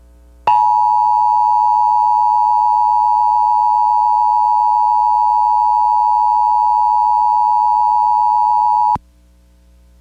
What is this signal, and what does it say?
Emergency Alert System (SAME) tones (the 853 and 960 Hz attention tone). It is heard from 0.4 to 9.0 s.